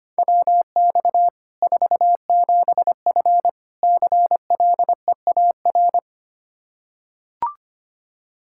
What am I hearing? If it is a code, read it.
Morse code, heard as a keyed tone, about 25 wpm, 700 Hz. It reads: WX 47F CLEAR
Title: Practice QSO - Exchange 2 - Info - 25wpm